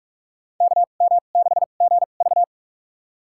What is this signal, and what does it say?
Morse code, heard as a keyed tone, about 46 wpm, spaced as for 33 wpm, 700 Hz. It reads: KM6GV
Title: USA Call Signs - 5 Characters - 2x Character Spacing - 45wpm